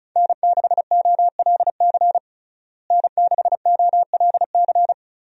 Morse code, heard as a keyed tone, about 34 wpm, 700 Hz. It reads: N6OLC N6OLC
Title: USA Call Signs - 5 Characters - Repeated 2x - 35wpm